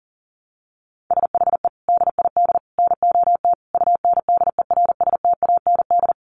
Morse code, heard as a keyed tone, about 40 wpm, 700 Hz. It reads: SHE DID NOT UNDERSTAND
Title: Sentences from Top 600 Words 40wpm